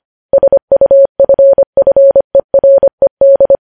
Morse code, heard as a keyed tone, about 25 wpm, 550 Hz. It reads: SUFFERED
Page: news-headlines-in-morse-code-at-25-wpm